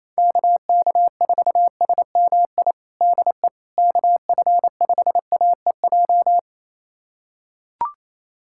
Morse code, heard as a keyed tone, about 28 wpm, 700 Hz. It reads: KK4SMI DE KF5AEJ
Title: USA Callsign DE Callsign - 6 Characters - 28wpm